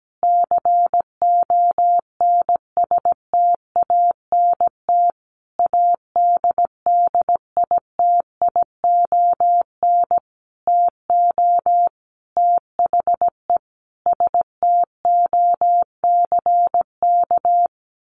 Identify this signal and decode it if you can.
Morse code, heard as a keyed tone, about 17 wpm, 700 Hz. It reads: CONSTANT ADDITION TO THE STOCK
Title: Sentences from Top 1500 Words 17wpm